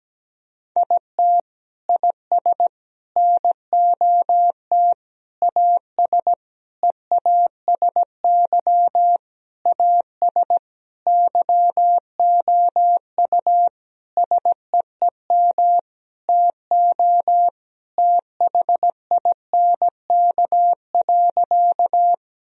Morse code, heard as a keyed tone, about 17 wpm, 700 Hz. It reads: IT IS NOT AS EASY AS YOU SEEM TO THINK.
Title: Sentences from Top 1000 Words - Review Part 1 - 17wpm